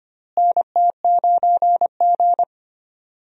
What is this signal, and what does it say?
Morse code, heard as a keyed tone, about 25 wpm, 700 Hz. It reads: NT9G
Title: USA Call Signs - 4 Characters - Repeated 2x - 25wpm